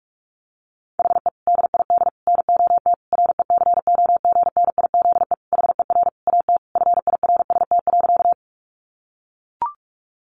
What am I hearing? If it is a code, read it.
Morse code, heard as a keyed tone, about 44 wpm, 700 Hz. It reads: HE DID NOT RECOGNIZE HER AT FIRST.
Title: Sentences from Top 1400 Words 45wpm